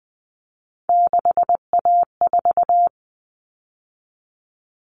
Morse code, heard as a keyed tone, about 20 wpm, 700 Hz. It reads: 6A4
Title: Common Cut Numbers - 3-Digit 20wpm